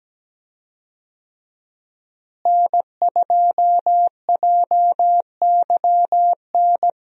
Morse code, heard as a keyed tone, about 17 wpm, 700 Hz. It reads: N2JYN
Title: USA Call Signs - 5 Characters - Repeated 3x - 17wpm